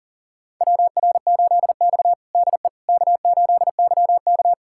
Morse code, heard as a keyed tone, about 37 wpm, 690 Hz. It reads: WR8X DE K8YK
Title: USA Callsign DE Callsign - 4 Characters - 40wpm